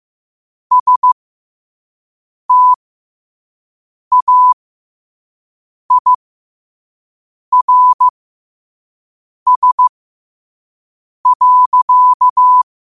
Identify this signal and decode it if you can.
Morse code, heard as a keyed tone, about 15 wpm, spaced as for 5 wpm, 1000 Hz. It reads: STAIRS.
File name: qotdfiles / 5wpm